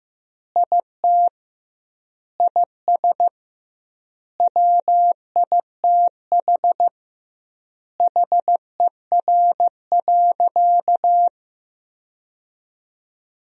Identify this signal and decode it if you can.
Morse code, heard as a keyed tone, about 16 wpm, 700 Hz. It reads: IT IS WITH HER.
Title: 2x Word-Spacing - Sentences from Top 100 Words - 15wpm